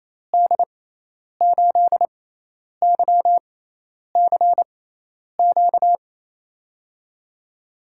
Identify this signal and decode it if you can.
Morse code, heard as a keyed tone, about 28 wpm, spaced as for 10 wpm, 700 Hz. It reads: D8YCQ
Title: USA Call Signs - 6 Characters - 6x Character Spacing - 28wpm